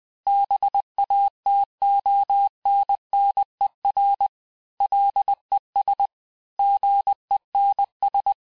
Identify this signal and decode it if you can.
Morse code, heard as a keyed tone, about 20 wpm, 800 Hz. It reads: BATONNER LES GENS
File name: cours-du-mercredi / audios / cours-858-texte-clais-vitesse-19mpm